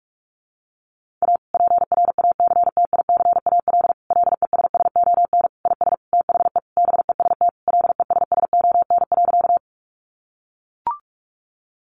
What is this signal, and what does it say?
Morse code, heard as a keyed tone, about 44 wpm, 700 Hz. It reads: A PRACTICAL LESSON IS THE BEST LESSON.
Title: Sentences from Top 1400 Words 45wpm